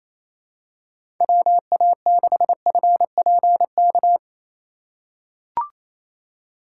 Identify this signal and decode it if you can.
Morse code, heard as a keyed tone, about 28 wpm, 700 Hz. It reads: WA6FPK
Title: USA Call Signs - 6 Characters - 4x Character Spacing - 28wpm